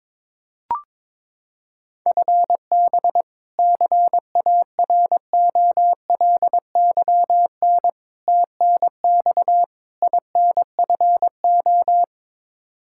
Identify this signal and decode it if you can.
Morse code, heard as a keyed tone, about 22 wpm, 700 Hz. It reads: FB CAROLYN TNX INFO
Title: Practice QSO - Exchange 3 - More Info - 22wpm